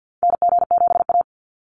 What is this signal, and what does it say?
Morse code, heard as a keyed tone, about 50 wpm, 700 Hz. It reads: NG7A
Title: NA SOTA Activator Callsigns 50wpm